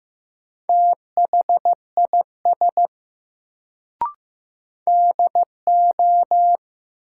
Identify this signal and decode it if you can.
Morse code, heard as a keyed tone, about 16 wpm, 700 Hz. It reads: THIS DO
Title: Sentences from Top 100 Words - New Word Drill - 15wpm